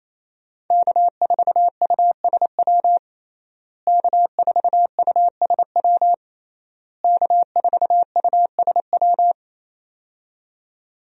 Morse code, heard as a keyed tone, about 27 wpm, 700 Hz. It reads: K4USW K4USW K4USW
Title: USA Call Signs - 5 Characters - Repeated 3x - 28wpm